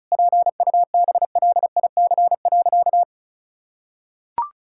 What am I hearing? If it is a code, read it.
Morse code, heard as a keyed tone, about 36 wpm, 700 Hz. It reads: PUBLIC.
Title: Sentences from Top 1000 Words - Review Part 5 - 35wpm